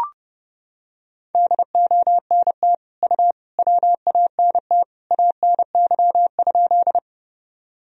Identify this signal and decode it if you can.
Morse code, heard as a keyed tone, about 32 wpm, 700 Hz. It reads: DONT U WANT ANY?
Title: Sentences from Top 300 Words and Abbreviations 30wpm